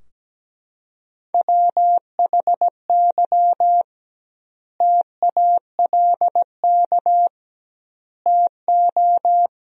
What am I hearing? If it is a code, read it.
Morse code, heard as a keyed tone, about 17 wpm, 700 Hz. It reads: WHY TALK TO
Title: 2x Word-Spacing - Sentences from Top 400 Words - 17wpm